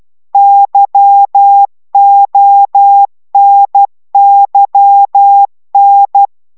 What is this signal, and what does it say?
Morse code, heard as a keyed tone, about 12 wpm, 800 Hz. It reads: YONYN